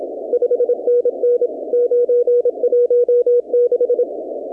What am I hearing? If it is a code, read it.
Morse code, heard as a keyed tone, about 28 wpm, 500 Hz. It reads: 5NN 916
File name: contests / 2019-cw-wpx-kt5j-k5pi- -2330 0-38